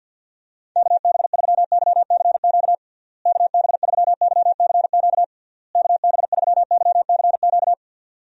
Morse code, heard as a keyed tone, about 50 wpm, 700 Hz. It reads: KB3YCQ KB3YCQ KB3YCQ
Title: USA Call Signs - 6 Characters - Repeated 3x - 50wpm